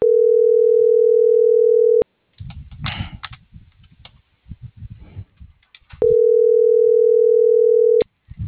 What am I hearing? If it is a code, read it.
Ringback tone (North America), 440 + 480 Hz.